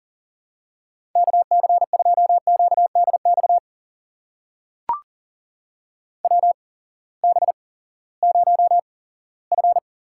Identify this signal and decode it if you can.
Morse code, heard as a keyed tone, about 39 wpm, spaced as for 20 wpm, 700 Hz. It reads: KC2QDX WB0F
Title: USA Call Signs - 6 Characters - 8x Character Spacing - 40wpm